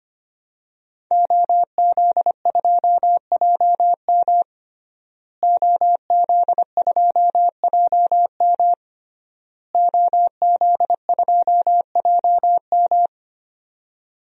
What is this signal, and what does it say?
Morse code, heard as a keyed tone, about 25 wpm, 700 Hz. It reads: OZ2JM OZ2JM OZ2JM
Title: DXCC-100 Call Signs - 5 Characters - Repeated 3x - 25wpm